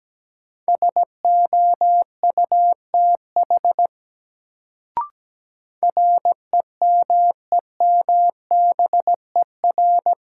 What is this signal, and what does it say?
Morse code, heard as a keyed tone, about 17 wpm, 700 Hz. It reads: SOUTH REMEMBER
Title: Sentences from Top 400 Words - New Word Drill - 17wpm